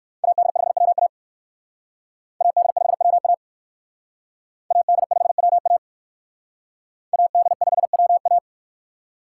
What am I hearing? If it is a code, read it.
Morse code, heard as a keyed tone, about 59 wpm, 700 Hz. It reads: AD5WA AD5WA AD5WA AD5WA